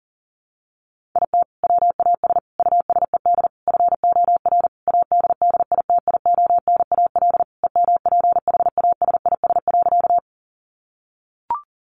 Morse code, heard as a keyed tone, about 40 wpm, 700 Hz. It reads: IT WAS USED FOR ADDITIONAL EMPHASIS.
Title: Sentences from Top 1400 Words 40wpm